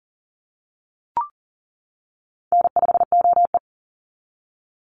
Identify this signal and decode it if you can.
Morse code, heard as a keyed tone, about 40 wpm, 700 Hz. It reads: N5OE
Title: Easy Call Signs - 40wpm